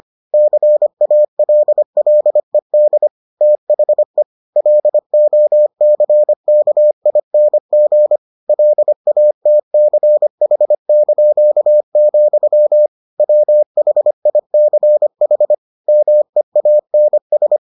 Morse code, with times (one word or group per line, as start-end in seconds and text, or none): 0.3-3.1 CALLED
3.4-4.2 THE
4.6-8.2 LOCKING
8.5-12.9 LATCH),
13.2-15.5 WHICH
15.9-17.6 MEANS